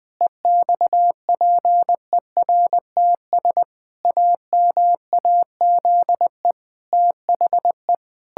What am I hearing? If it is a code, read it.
Morse code, heard as a keyed tone, about 20 wpm, 700 Hz. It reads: EXPERTS AMAZE THE
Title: Pangram Challenge v2 - 20wpm